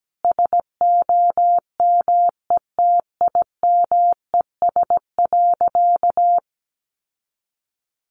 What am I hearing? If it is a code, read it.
Morse code, heard as a keyed tone, about 17 wpm, 700 Hz. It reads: SOMETIMES.
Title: Sentences from Top 600 Words 17wpm